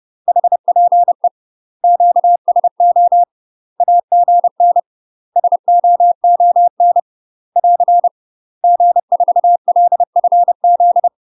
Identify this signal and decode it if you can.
Morse code, heard as a keyed tone, about 30 wpm, 700 Hz. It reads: HPE QSO AGN SOON <AR> G4LFZ
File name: UK MASTER QSOs TRAINING / 30WPM / UK MASTER QSOs 0004